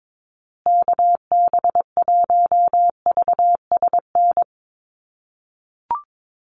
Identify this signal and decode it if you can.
Morse code, heard as a keyed tone, about 22 wpm, 700 Hz. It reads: KB1VSN
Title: USA Call Signs Repeated 2x 22wpm